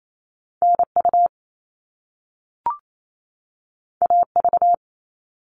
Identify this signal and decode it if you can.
Morse code, heard as a keyed tone, about 28 wpm, 700 Hz. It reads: NU AV
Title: All Cut Numbers - 2-Digit 28wpm